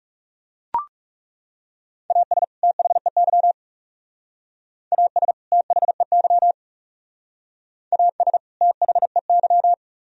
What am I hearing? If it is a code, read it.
Morse code, heard as a keyed tone, about 37 wpm, 700 Hz. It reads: AS THEY AS THEY AS THEY
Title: Speed-Racing - Sets of 2 Words 30wpm